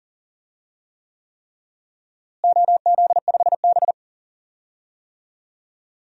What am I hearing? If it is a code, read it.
Morse code, heard as a keyed tone, about 40 wpm, 700 Hz. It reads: OZ5B
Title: DXCC-100 Call Signs - 4 Characters - Repeated 3x - 40wpm